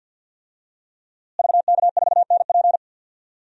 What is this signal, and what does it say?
Morse code, heard as a keyed tone, about 46 wpm, 700 Hz. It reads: VK4NP
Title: DXCC-100 Call Signs - 5 Characters - Repeated 3x - 50wpm